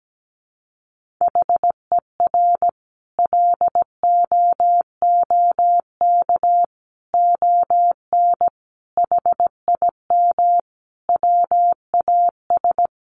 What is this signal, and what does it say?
Morse code, heard as a keyed tone, about 17 wpm, 700 Hz. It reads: HER LOOK ON HIM WAS
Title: Sentences from Top 1400 Words 17wpm